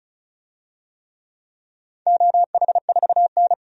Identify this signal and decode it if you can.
Morse code, heard as a keyed tone, about 36 wpm, 700 Hz. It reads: OH4N
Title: DXCC-100 Call Signs - 4 Characters - 35wpm